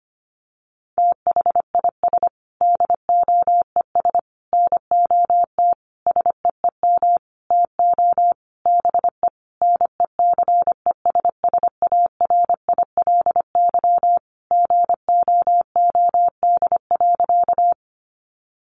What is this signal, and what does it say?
Morse code, heard as a keyed tone, about 25 wpm, 700 Hz. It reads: THIS DOES NOT SEEM TO BE NECESSARILY GOOD.
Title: Sentences from Top 1400 Words 25wpm